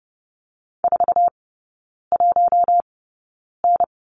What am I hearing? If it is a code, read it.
Morse code, heard as a keyed tone, about 30 wpm, spaced as for 9 wpm, 700 Hz. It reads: 41N
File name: Lesson 16e - Letter c - Three Characters 30wpm